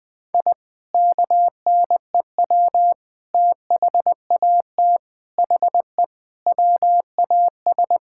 Morse code, heard as a keyed tone, about 20 wpm, 700 Hz. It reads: I KNEW THAT HE WAS
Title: Sentences with 3-Word Phrases - Spoken Beginning - 20wpm